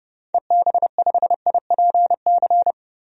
Morse code, heard as a keyed tone, about 30 wpm, 700 Hz. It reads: EB5IPC
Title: DXCC-100 Call Signs - 6 Characters - 30wpm